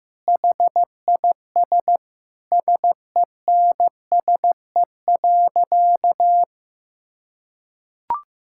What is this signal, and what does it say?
Morse code, heard as a keyed tone, about 15 wpm, 700 Hz. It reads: HIS SENSE.
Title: Sentences from Top 1600 Words 15wpm